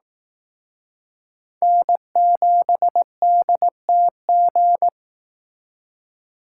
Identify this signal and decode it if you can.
Morse code, heard as a keyed tone, about 18 wpm, 700 Hz. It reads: N7DTG
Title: Speed Racing - USA Call Signs 15wpm